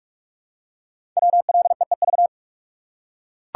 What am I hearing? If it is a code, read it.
Morse code, heard as a keyed tone, about 44 wpm, 700 Hz. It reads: WLEEV
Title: Code Groups - 5 Letters - 45wpm